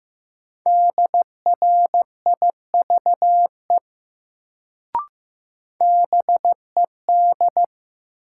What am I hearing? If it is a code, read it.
Morse code, heard as a keyed tone, about 15 wpm, 700 Hz. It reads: DRIVE BED
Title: Sentences from Top 500 Words - New Word Drill - 15wpm